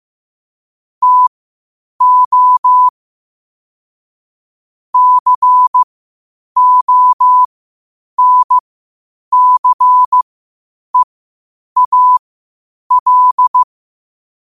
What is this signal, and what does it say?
Morse code, heard as a keyed tone, about 15 wpm, spaced as for 8 wpm, 1000 Hz. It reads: TO CONCEAL
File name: qotdfiles / 8wpm